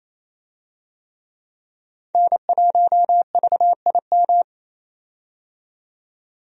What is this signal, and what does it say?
Morse code, heard as a keyed tone, about 27 wpm, 700 Hz. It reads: N1VIM
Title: USA Call Signs - 5 Characters - 28wpm